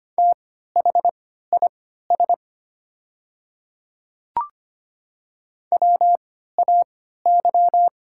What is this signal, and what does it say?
Morse code, heard as a keyed tone, about 25 wpm, spaced as for 14 wpm, 700 Hz. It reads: THIS WAY
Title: 3x Character Spacing - Top 100 Words - 25wpm